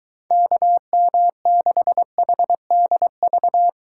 Morse code, heard as a keyed tone, about 21 wpm, 700 Hz. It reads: KM6HDV